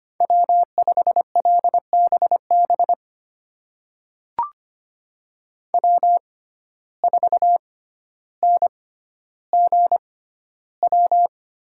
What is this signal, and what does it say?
Morse code, heard as a keyed tone, about 25 wpm, spaced as for 13 wpm, 700 Hz. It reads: W5LBB W4NGW
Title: USA Call Signs - 5 Characters - 6x Character Spacing - 25wpm